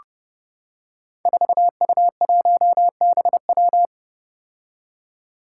Morse code, heard as a keyed tone, about 30 wpm, 700 Hz. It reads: 4U1BW
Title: Five Challenging Letter-Number Combinations 30wpm